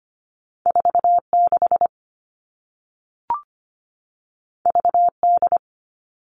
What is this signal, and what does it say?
Morse code, heard as a keyed tone, about 25 wpm, 700 Hz. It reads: 46 VD
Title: All Cut Numbers - 2-Digit 25wpm